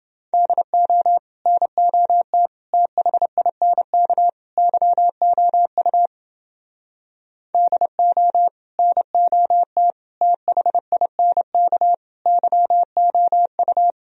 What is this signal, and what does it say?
Morse code, heard as a keyed tone, about 29 wpm, 700 Hz. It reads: DO NOT THINK YOU DO NOT THINK YOU